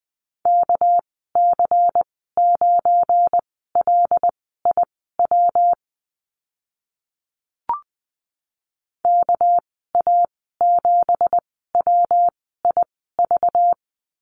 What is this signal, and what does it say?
Morse code, heard as a keyed tone, about 19 wpm, spaced as for 14 wpm, 700 Hz. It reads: KC9LIW KA7WIV
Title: 200 Lesson 32g - Letter k - Call Signs 20wpm